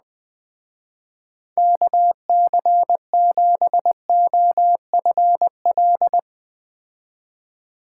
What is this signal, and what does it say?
Morse code, heard as a keyed tone, about 20 wpm, 700 Hz. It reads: KC7OFL
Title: Speed Racing - USA 2x3 Call Signs 17wpm